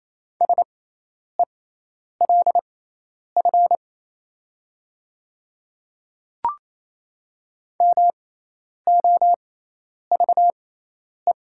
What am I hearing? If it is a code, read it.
Morse code, heard as a keyed tone, about 27 wpm, spaced as for 10 wpm, 700 Hz. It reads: SELF MOVE
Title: 6x Character Spacing - Top 200 Words - 28wpm